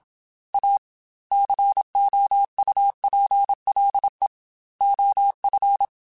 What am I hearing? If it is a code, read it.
Morse code, heard as a keyed tone, about 27 wpm, 800 Hz. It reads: A COUPLE OF
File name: Technical Subjects / Technical Subjects-mp3s / Sw Pwr Sup @25